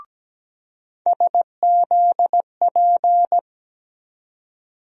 Morse code, heard as a keyed tone, about 17 wpm, 700 Hz. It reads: SZP
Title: USA Call Sign Postfixes - 3 Letters 17wpm